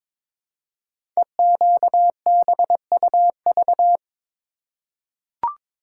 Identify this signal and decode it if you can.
Morse code, heard as a keyed tone, about 22 wpm, 700 Hz. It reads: EQBUV
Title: Code Groups - 5 Letters - 22wpm